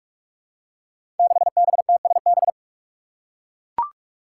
Morse code, heard as a keyed tone, about 44 wpm, 700 Hz. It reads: 6BTSB
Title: Five Challenging Letter-Number Combinations 45wpm